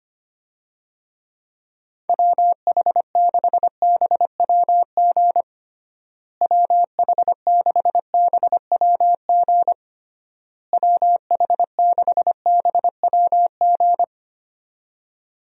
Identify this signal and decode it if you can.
Morse code, heard as a keyed tone, about 25 wpm, 700 Hz. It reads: WH6BWG WH6BWG WH6BWG
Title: USA Call Signs - 6 Characters - Repeated 3x - 25wpm